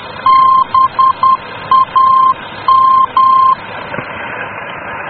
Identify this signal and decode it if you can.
Morse code, heard as a keyed tone, about 10 wpm, 1050 Hz. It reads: BAM
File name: germany / mx / BAM-D